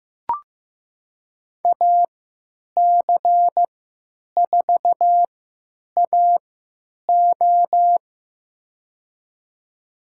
Morse code, heard as a keyed tone, about 15 wpm, spaced as for 9 wpm, 700 Hz. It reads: AC4AO
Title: USA Call Signs - 5 Characters - 3x Character Spacing - 15wpm